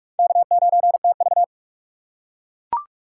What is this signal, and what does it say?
Morse code, heard as a keyed tone, about 42 wpm, 700 Hz. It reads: K9TV